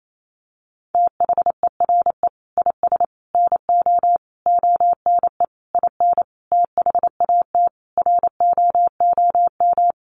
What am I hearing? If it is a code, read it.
Morse code, heard as a keyed tone, about 28 wpm, 700 Hz. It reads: THERE IS NO ONE IN THAT ROOM
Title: Sentences from Top 500 Words 28wpm